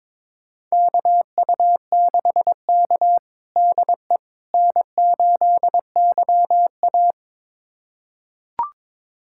Morse code, heard as a keyed tone, about 22 wpm, 700 Hz. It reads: KU6K DE N8YA
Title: USA Callsign DE Callsign - 4 Characters - 22wpm